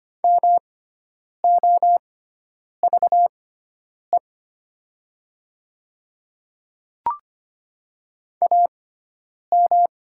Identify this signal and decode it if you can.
Morse code, heard as a keyed tone, about 25 wpm, spaced as for 9 wpm, 700 Hz. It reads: MOVE AM
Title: 6x Character Spacing - Top 500 Words - 25wpm